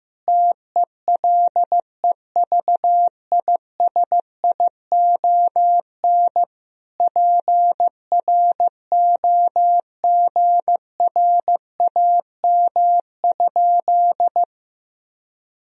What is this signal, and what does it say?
Morse code, heard as a keyed tone, about 15 wpm, 700 Hz. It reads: TELEVISION PROGRAM?
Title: Sentences from Top 1000 Words - Review Part 4 - 15wpm